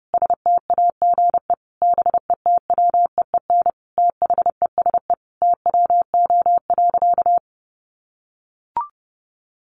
Morse code, heard as a keyed tone, about 30 wpm, 700 Hz. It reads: STAGE BETWEEN THESE TWO.
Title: Sentences from Top 1200 Words 30wpm